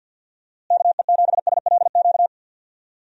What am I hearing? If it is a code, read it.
Morse code, heard as a keyed tone, about 46 wpm, 700 Hz. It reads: KE7SLQ
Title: USA Call Signs - 6 Characters - Repeated 3x - 50wpm